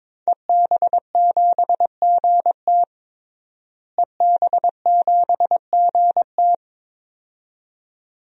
Morse code, heard as a keyed tone, about 22 wpm, 700 Hz. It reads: EB7GT EB7GT
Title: DXCC-100 Call Signs - 5 Characters - Repeated 2x - 22wpm